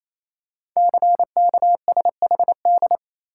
Morse code, heard as a keyed tone, about 28 wpm, 690 Hz. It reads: CKSHD
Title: Code Groups - 5 Letters - 28wpm